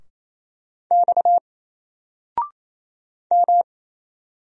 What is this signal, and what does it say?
Morse code, heard as a keyed tone, about 28 wpm, 700 Hz. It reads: X M
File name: Single-Letter-Number-28wpm